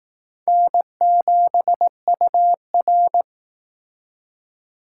Morse code, heard as a keyed tone, about 18 wpm, 700 Hz. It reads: N7UR
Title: Speed Racing - USA Call Signs 15wpm